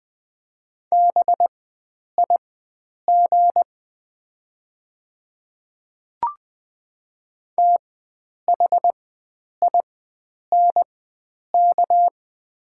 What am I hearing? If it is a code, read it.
Morse code, heard as a keyed tone, about 20 wpm, spaced as for 9 wpm, 700 Hz. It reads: BIG THINK
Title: 4x Character Spacing - Top 500 Words - 20wpm